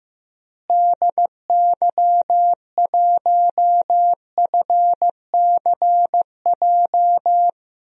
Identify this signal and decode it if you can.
Morse code, heard as a keyed tone, about 15 wpm, 700 Hz. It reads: DY1FCJ
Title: DXCC-100 Call Signs - 6 Characters - Repeated 3x - 15wpm